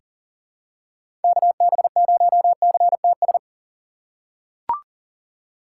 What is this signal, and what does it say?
Morse code, heard as a keyed tone, about 40 wpm, 700 Hz. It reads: KB0CTS
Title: USA Call Signs - 6 Characters - 6x Character Spacing - 40wpm